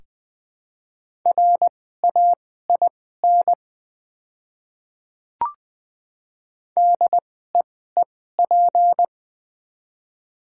Morse code, heard as a keyed tone, about 20 wpm, spaced as for 14 wpm, 700 Hz. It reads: RAIN DEEP
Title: 2x Character Spacing - Top 500 Words - 20wpm